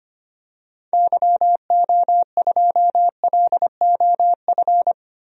Morse code, heard as a keyed tone, about 25 wpm, 700 Hz. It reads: YO2LOF
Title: DXCC-100 Call Signs - 6 Characters - 25wpm